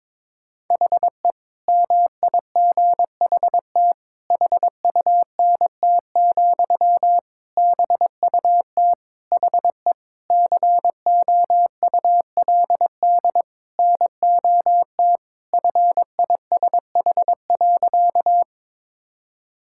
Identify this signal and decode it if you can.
Morse code, heard as a keyed tone, about 22 wpm, 700 Hz. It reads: HE MIGHT HUNT, BUT HE COULD NOT FISH.
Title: Sentences from Top 1600 Words 22wpm